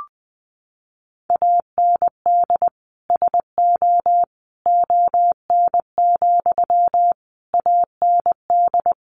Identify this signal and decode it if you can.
Morse code, heard as a keyed tone, about 20 wpm, 700 Hz. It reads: AND SO ON, AND
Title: Sentences from Top 100 Words 20wpm